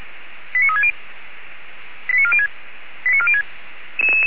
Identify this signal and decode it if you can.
Five-tone selective call (ZVEI), heard as quick runs of five tones, tones of about 70 ms. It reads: 89360 89406 89406